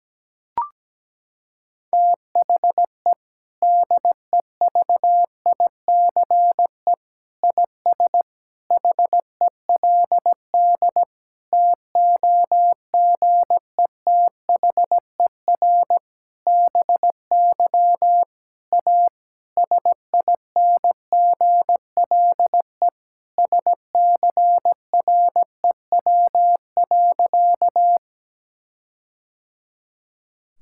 Morse code, heard as a keyed tone, about 17 wpm, 700 Hz. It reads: THE DEVICE IS HELD TOGETHER BY A SINGLE SCREW.
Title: Sentences from Top 1600 Words 17wpm